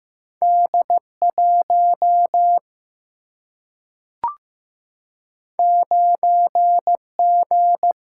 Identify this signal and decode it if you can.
Morse code, heard as a keyed tone, about 15 wpm, 700 Hz. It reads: D 1 9 G
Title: Single-Letter Single-Number 15wpm